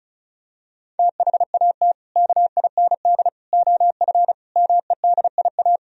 Morse code, heard as a keyed tone, about 36 wpm, 700 Hz. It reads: THAT KIND OF MEDIA
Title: Sentences from Top 1000 Words - Review Part 4 - 35wpm